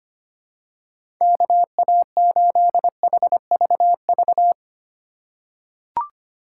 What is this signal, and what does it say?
Morse code, heard as a keyed tone, about 25 wpm, 700 Hz. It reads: KA8HVV
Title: USA Call Signs - 3x Character Spacing - 25wpm